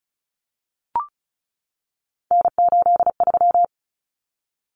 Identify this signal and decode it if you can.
Morse code, heard as a keyed tone, about 34 wpm, 700 Hz. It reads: N83
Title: Common Cut Numbers - 3-Digit 35wpm